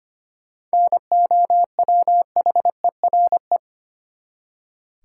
Morse code, heard as a keyed tone, about 25 wpm, 700 Hz. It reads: NOWHERE
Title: Seven Letter Words - 25wpm